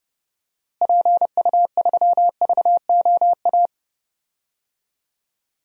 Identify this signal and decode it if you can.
Morse code, heard as a keyed tone, about 30 wpm, 700 Hz. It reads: PU3VOA
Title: DXCC-100 Call Signs - 6 Characters - Repeated 3x - 30wpm